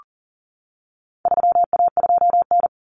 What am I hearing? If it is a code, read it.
Morse code, heard as a keyed tone, about 40 wpm, 700 Hz. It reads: 3A2N
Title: Common Cut Numbers - 4-Digit 40wpm